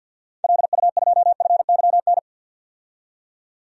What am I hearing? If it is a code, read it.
Morse code, heard as a keyed tone, about 52 wpm, 700 Hz. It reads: LU2FYN
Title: DXCC-100 Call Signs - 6 Characters - 50wpm